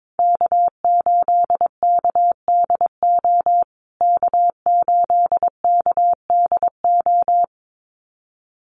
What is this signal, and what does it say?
Morse code, heard as a keyed tone, about 22 wpm, 700 Hz. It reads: K8KDO K8KDO
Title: USA Call Signs Repeated 2x 22wpm